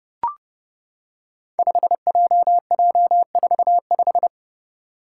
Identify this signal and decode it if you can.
Morse code, heard as a keyed tone, about 30 wpm, 700 Hz. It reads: 5JJ45
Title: Five Challenging Letter-Number Combinations 30wpm